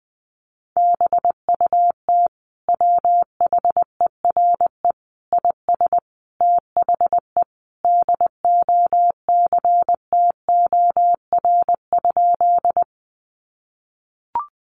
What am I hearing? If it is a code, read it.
Morse code, heard as a keyed tone, about 20 wpm, 700 Hz. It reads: BUT WHERE IS THE DOCTOR?
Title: Sentences from Top 1500 Words 20wpm